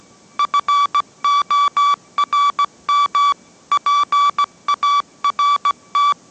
Morse code, heard as a keyed tone, about 17 wpm, 1150 Hz. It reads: FORM PART
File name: mrx / mrx080 18wpm